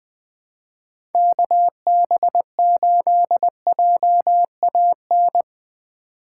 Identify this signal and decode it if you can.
Morse code, heard as a keyed tone, about 22 wpm, 700 Hz. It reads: KB8JAN
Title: Speed Racing - USA 2x3 Call Signs 20wpm